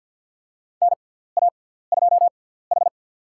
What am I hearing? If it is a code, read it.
Morse code, heard as a keyed tone, about 53 wpm, spaced as for 17 wpm, 700 Hz. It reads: NA2H
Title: USA Call Signs - 6x Character Spacing - 50wpm